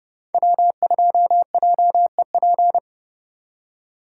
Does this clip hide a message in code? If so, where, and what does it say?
Yes: Morse code from 0.3 to 2.8 s, heard as a keyed tone, about 29 wpm, 700 Hz. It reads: W2JEP